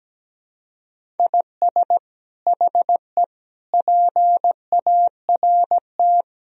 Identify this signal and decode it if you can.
Morse code, heard as a keyed tone, about 17 wpm, 700 Hz. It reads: IS HE PART
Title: Sentences from Top 1000 Words - Review Part 2 - 17wpm